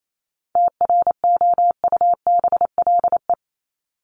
Morse code, heard as a keyed tone, about 28 wpm, 700 Hz. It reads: TROUBLE
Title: Top 1001-1250 Words 28wpm